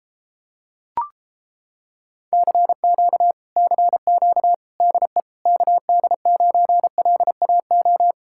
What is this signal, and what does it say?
Morse code, heard as a keyed tone, about 31 wpm, 700 Hz. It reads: CQ CQ DE KD9LAO